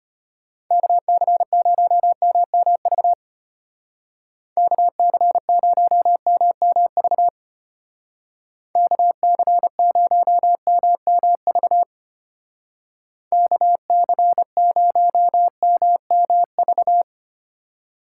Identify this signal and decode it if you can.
Morse code, heard as a keyed tone, about 32 wpm, 700 Hz. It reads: KC0MMV KC0MMV KC0MMV KC0MMV